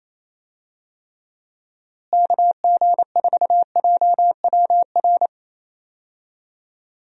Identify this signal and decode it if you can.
Morse code, heard as a keyed tone, about 28 wpm, 700 Hz. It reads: KG4JWR
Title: USA Call Signs - 6 Characters - Repeated 2x - 28wpm